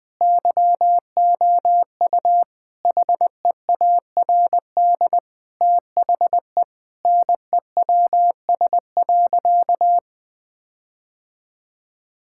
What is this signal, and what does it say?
Morse code, heard as a keyed tone, about 20 wpm, 700 Hz. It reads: YOU HEARD THE NEWS.
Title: Sentences with 5-Word Phrases - 20wpm